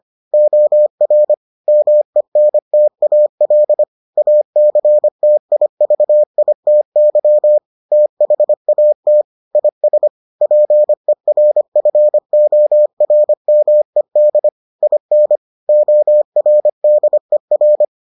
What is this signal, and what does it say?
Morse code, heard as a keyed tone, about 25 wpm, 600 Hz. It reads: OR MENTAL ACTIVITY THAT IS PERFORMED IN ORDER